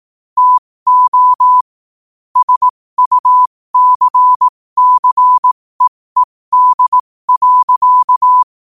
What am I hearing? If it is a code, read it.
Morse code, heard as a keyed tone, about 18 wpm, spaced as for 15 wpm, 1000 Hz. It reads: TO SUCCEED.
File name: qotdfiles / 15wpm